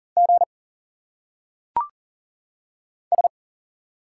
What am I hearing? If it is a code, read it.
Morse code, heard as a keyed tone, about 40 wpm, 700 Hz. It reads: G S
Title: Single Letters - Phonetic Pronunciation 40wpm